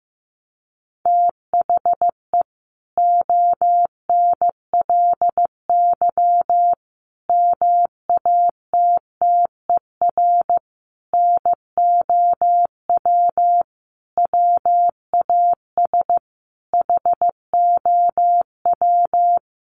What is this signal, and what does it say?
Morse code, heard as a keyed tone, about 15 wpm, 700 Hz. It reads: THE ONLY MATTER NOW WAS HOW
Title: Sentences from Top 1400 Words 15wpm